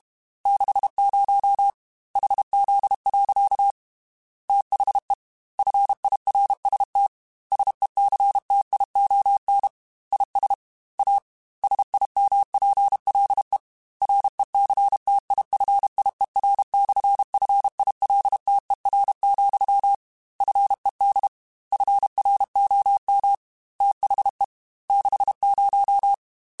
Morse code, heard as a keyed tone, about 32 wpm, 800 Hz. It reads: 60 HZ. THE FIRST SECTION IS A SIMPLE RECTIFIER/FILTER, FED FROM THE 60
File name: Technical Subjects / Technical Subjects-mp3s / Sw Pwr Sup @30